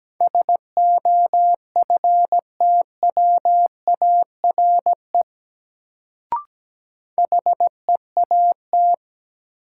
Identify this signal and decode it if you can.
Morse code, heard as a keyed tone, about 17 wpm, 700 Hz. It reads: SOFTWARE HEAT
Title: Sentences from Top 1000 Words - Review Part 3 - New Word Drill - 17wpm